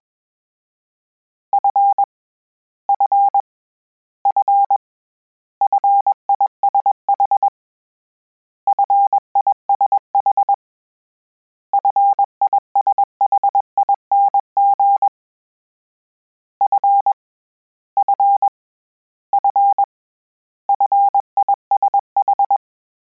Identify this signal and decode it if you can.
Morse code, heard as a keyed tone, about 21 wpm, 800 Hz. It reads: F F F FISH FISH FISHING F F F FISH
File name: Combinations / Combinations-mp3s / Combinations@20